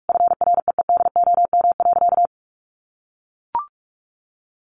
Morse code, heard as a keyed tone, about 44 wpm, 700 Hz. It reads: FREEDOM.
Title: Sentences from Top 1100 Words 45wpm